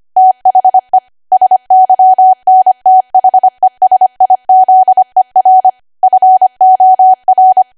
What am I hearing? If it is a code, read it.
Morse code, heard as a keyed tone, about 25 wpm, 750 Hz. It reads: THE SYNTHESIZER FOR
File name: Archive / 25 WPM / 25WPM